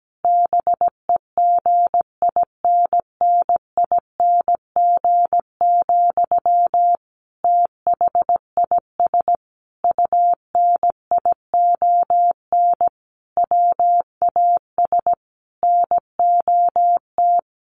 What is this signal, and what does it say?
Morse code, heard as a keyed tone, about 17 wpm, 700 Hz. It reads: BEGINNING, THIS UNION WAS NOT
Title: Sentences from Top 1400 Words 17wpm